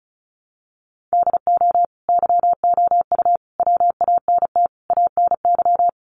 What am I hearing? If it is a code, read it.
Morse code, heard as a keyed tone, about 34 wpm, 700 Hz. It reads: DO YOU WANT ANY
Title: Sentences from Top 400 Words 35wpm